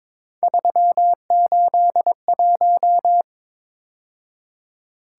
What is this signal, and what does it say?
Morse code, heard as a keyed tone, about 26 wpm, 700 Hz. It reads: 381
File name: Speed-Racing - Three Letter-Number Combinations - 22wpm